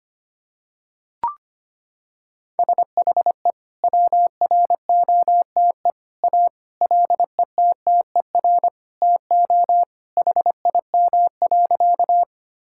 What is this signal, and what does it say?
Morse code, heard as a keyed tone, about 25 wpm, 700 Hz. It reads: SHE WROTE A LETTER TO HIM.
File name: practice / 25wpm / Sentences with 2-Word Phrases - 25wpm